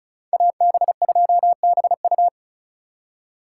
Morse code, heard as a keyed tone, about 46 wpm, 700 Hz. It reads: AB2BU